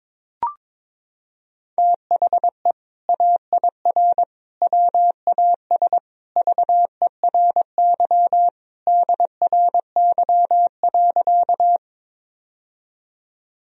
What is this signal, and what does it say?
Morse code, heard as a keyed tone, about 22 wpm, 700 Hz. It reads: THE AIR WAS VERY DRY.
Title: Sentences from Top 1000 Words - Review Part 3 - 22wpm